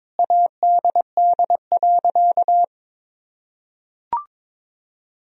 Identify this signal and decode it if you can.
Morse code, heard as a keyed tone, about 22 wpm, 700 Hz. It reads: ADD.
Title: Sentences from Top 1000 Words - Review Part 1 - 22wpm